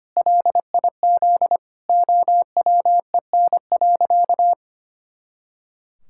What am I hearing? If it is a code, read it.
Morse code, heard as a keyed tone, about 25 wpm, 700 Hz. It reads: LIZ OWEN.
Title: Pangram Challenge v2 - 25wpm